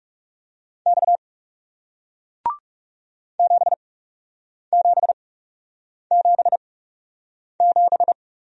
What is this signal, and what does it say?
Morse code, heard as a keyed tone, about 40 wpm, spaced as for 8 wpm, 700 Hz. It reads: X 7777